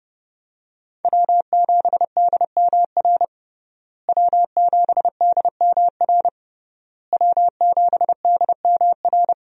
Morse code, heard as a keyed tone, about 32 wpm, 700 Hz. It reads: W7DMR W7DMR W7DMR
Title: USA Call Signs - 5 Characters - Repeated 3x - 30wpm